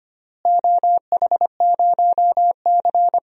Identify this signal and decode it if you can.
Morse code, heard as a keyed tone, about 25 wpm, 700 Hz. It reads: OH0C
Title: DXCC-100 Call Signs - 4 Characters - Repeated 2x - 25wpm